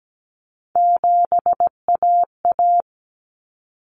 Morse code, heard as a keyed tone, about 17 wpm, 700 Hz. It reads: 7AA
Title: Common Cut Numbers - 3-Digit 17wpm